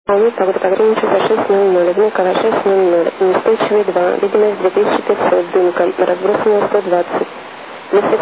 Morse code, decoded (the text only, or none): none